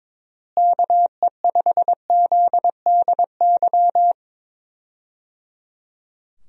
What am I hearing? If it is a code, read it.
Morse code, heard as a keyed tone, about 22 wpm, 700 Hz. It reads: KE5ZDY
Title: USA Call Signs - 6 Characters - Repeated 3x - 22wpm